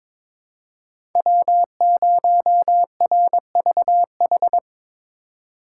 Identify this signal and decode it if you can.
Morse code, heard as a keyed tone, about 22 wpm, 700 Hz. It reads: W0RVH
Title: USA Call Signs - 2x Character Spacing - 22wpm